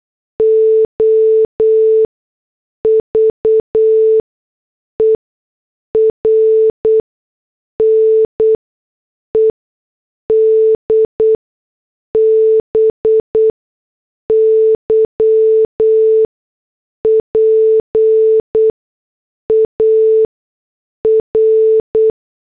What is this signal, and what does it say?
Morse code, heard as a keyed tone, about 8 wpm, 440 Hz. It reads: O V E R N E D B Y P A R
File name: morse / headlines